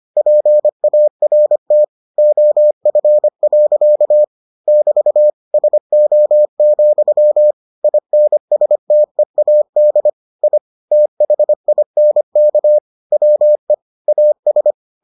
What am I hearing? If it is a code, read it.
Morse code, heard as a keyed tone, about 25 wpm, 600 Hz. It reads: PART OF. <BT> SO, INSTEAD I THINK WE AS